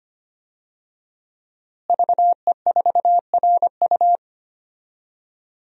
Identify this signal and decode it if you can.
Morse code, heard as a keyed tone, about 25 wpm, 700 Hz. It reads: VE4RU
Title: DXCC-100 Call Signs - 5 Characters - Repeated 2x - 25wpm